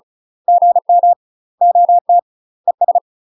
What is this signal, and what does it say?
Morse code, heard as a keyed tone, about 36 wpm, 700 Hz. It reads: GM OT ES
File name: UK MASTER QSOs TRAINING / 35WPM / UK MASTER QSOs 0018